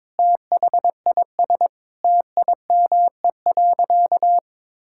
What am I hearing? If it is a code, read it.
Morse code, heard as a keyed tone, about 22 wpm, 700 Hz. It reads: THIS TIME.
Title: Sentences from Top 1000 Words - Review Part 5 - 22wpm